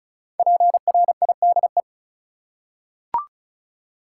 Morse code, heard as a keyed tone, about 34 wpm, 700 Hz. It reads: PRIDE
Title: Five Letter Words - 35wpm